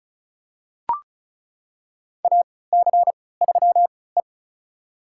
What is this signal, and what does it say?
Morse code, heard as a keyed tone, about 36 wpm, spaced as for 20 wpm, 700 Hz. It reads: AC3E
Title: USA Call Signs - 4 Characters - 3x Character Spacing - 35wpm